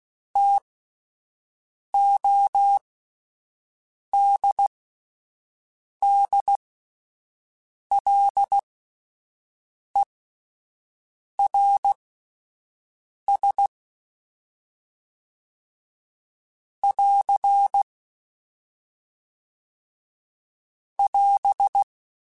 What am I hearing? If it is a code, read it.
Morse code, heard as a keyed tone, about 16 wpm, spaced as for 5 wpm, 800 Hz. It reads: TODDLERS <AR> <AS>